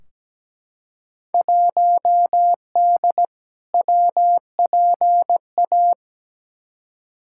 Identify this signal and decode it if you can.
Morse code, heard as a keyed tone, about 17 wpm, 700 Hz. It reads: 1D WPA
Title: Field Day Exchanges - 17wpm